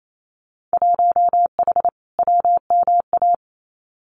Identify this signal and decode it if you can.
Morse code, heard as a keyed tone, about 28 wpm, 700 Hz. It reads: 1H WMA
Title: Winter Field Day Exchanges 28wpm